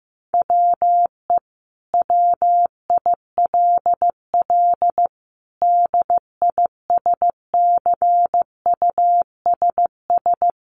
Morse code, heard as a keyed tone, about 14 wpm, 700 Hz. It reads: WE WILL DISCUSS
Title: Sentences from Top 1000 Words 15wpm